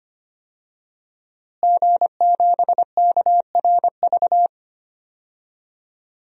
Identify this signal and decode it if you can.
Morse code, heard as a keyed tone, about 25 wpm, 700 Hz. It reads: G7KRV
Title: DXCC-100 Call Signs - 5 Characters - Repeated 2x - 25wpm